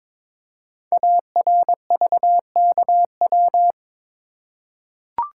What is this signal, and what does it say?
Morse code, heard as a keyed tone, about 23 wpm, 700 Hz. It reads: ARVKW
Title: Code Groups - 5 Letters - 22wpm